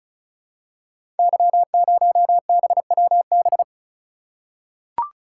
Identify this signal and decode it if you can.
Morse code, heard as a keyed tone, about 36 wpm, 700 Hz. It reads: Y0BWB
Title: Code Groups - 5 Letter:Numbers - 35wpm